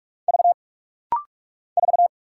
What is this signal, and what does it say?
Morse code, heard as a keyed tone, about 44 wpm, 700 Hz. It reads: V 4
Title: Single Letter-Number - Rapid-Fire 45wpm